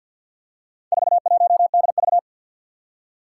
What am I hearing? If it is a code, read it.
Morse code, heard as a keyed tone, about 52 wpm, 700 Hz. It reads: 41DV
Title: Four Challenging Letter-Number Combinations 50wpm